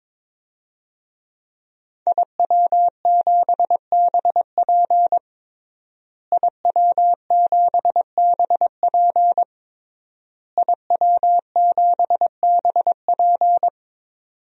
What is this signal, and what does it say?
Morse code, heard as a keyed tone, about 22 wpm, 700 Hz. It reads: IW7BP IW7BP IW7BP
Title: DXCC-100 Call Signs - 5 Characters - Repeated 3x - 22wpm